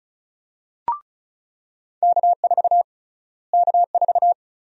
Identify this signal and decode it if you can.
Morse code, heard as a keyed tone, about 34 wpm, 700 Hz. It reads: K4 K4
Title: USA Call Sign Prefixes - Repeated 2x - 35wpm